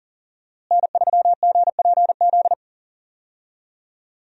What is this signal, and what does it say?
Morse code, heard as a keyed tone, about 40 wpm, 700 Hz. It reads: N3GPZ
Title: USA Call Signs - 5 Characters - Repeated 2x - 40wpm